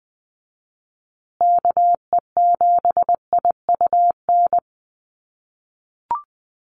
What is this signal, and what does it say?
Morse code, heard as a keyed tone, about 20 wpm, 700 Hz. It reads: KE7IUN
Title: USA Call Signs Repeated 3x 20wpm